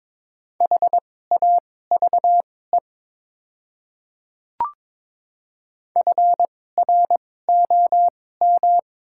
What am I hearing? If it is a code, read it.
Morse code, heard as a keyed tone, about 22 wpm, spaced as for 16 wpm, 690 Hz. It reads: HAVE FROM
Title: 2x Character Spacing - Top 100 Words - 22wpm